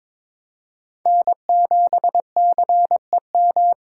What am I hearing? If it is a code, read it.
Morse code, heard as a keyed tone, about 23 wpm, 700 Hz. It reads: N7CEM
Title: USA Call Signs - 5 Characters - 4x Character Spacing - 22wpm